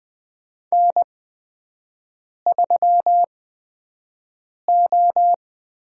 Morse code, heard as a keyed tone, about 21 wpm, spaced as for 5 wpm, 700 Hz. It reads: N3O
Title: USA Call Signs - 4 Characters - 8x Character Spacing - 20wpm